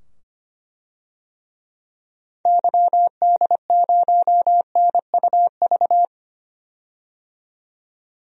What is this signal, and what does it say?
Morse code, heard as a keyed tone, about 25 wpm, 700 Hz. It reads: YD0NUV
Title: DXCC-100 Call Signs - 6 Characters - 25wpm